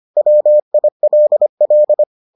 Morse code, heard as a keyed tone, about 25 wpm, 600 Hz. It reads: WILL